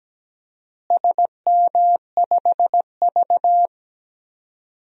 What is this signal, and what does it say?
Morse code, heard as a keyed tone, about 17 wpm, 700 Hz. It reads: SM5V
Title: DXCC-100 Call Signs - 4 Characters - Repeated 2x - 17wpm